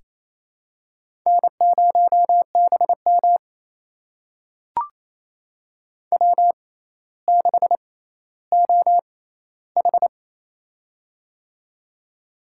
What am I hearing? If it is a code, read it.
Morse code, heard as a keyed tone, about 28 wpm, spaced as for 14 wpm, 700 Hz. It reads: N0BM W6OH
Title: USA Call Signs - 4 Characters - 6x Character Spacing - 28wpm